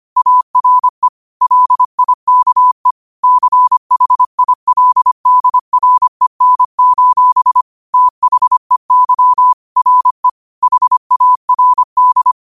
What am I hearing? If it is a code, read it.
Morse code, heard as a keyed tone, about 25 wpm, 1000 Hz. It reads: ARE LIKE CHILDREN: THEY RE HARD